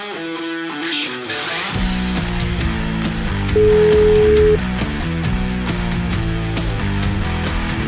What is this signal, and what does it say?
Ringback tone (Europe, 425 Hz), with noise.